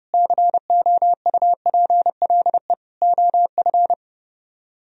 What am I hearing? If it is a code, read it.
Morse code, heard as a keyed tone, about 36 wpm, 700 Hz. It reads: COUPLE OF
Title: Speed-Racing - Sets of 4 Words 30wpm